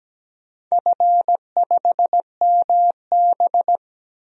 Morse code, heard as a keyed tone, about 17 wpm, 700 Hz. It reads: F5MB
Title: DXCC-100 Call Signs - 4 Characters - 17wpm